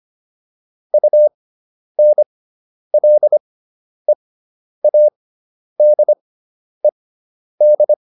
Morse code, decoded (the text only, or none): UNLEADED